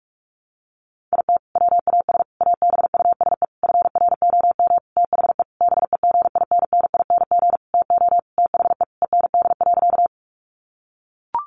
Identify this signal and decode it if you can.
Morse code, heard as a keyed tone, about 44 wpm, 700 Hz. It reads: IT WAS ABUSE FROM THE BEGINNING TO THE END.
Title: Sentences from Top 1300 Words 45wpm